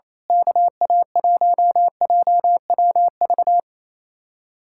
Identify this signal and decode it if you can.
Morse code, heard as a keyed tone, about 27 wpm, 700 Hz. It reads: KA1JWV
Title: Speed-Racing Common QSO Phrases - 28wpm